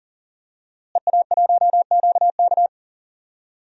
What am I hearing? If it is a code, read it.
Morse code, heard as a keyed tone, about 40 wpm, 700 Hz. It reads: EA1QK
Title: DXCC-100 Call Signs - 5 Characters - Repeated 3x - 40wpm